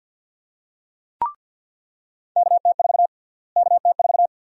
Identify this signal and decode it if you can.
Morse code, heard as a keyed tone, about 49 wpm, 700 Hz. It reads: KT4 KT4
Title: USA Call Sign Prefixes - Repeated 2x - 50wpm